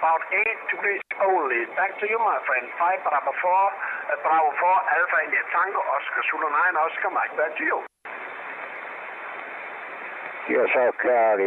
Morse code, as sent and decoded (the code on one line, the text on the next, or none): none